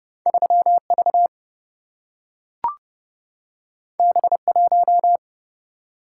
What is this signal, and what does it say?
Morse code, heard as a keyed tone, about 29 wpm, 700 Hz. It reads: 3V B1
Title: Single-Letter Single-Number 30wpm